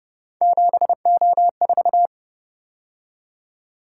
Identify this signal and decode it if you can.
Morse code, heard as a keyed tone, about 30 wpm, 700 Hz. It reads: 7O4
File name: Speed-Racing - Three Letter-Number Combinations - 25wpm